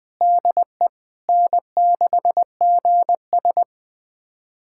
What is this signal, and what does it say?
Morse code, heard as a keyed tone, about 20 wpm, 700 Hz. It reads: DE N6GS
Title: USA Callsign DE Callsign - 4 Characters - 20wpm